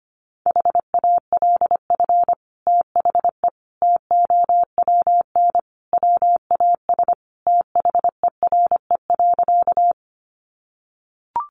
Morse code, heard as a keyed tone, about 25 wpm, 700 Hz. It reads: HALF THE TOWN WAS THERE.
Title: Sentences from Top 500 Words 25wpm